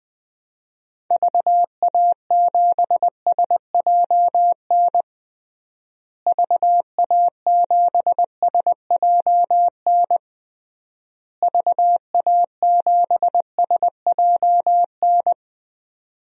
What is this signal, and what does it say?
Morse code, heard as a keyed tone, about 21 wpm, 700 Hz. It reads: VA7SJN VA7SJN VA7SJN
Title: DXCC-100 Call Signs - 6 Characters - Repeated 3x - 20wpm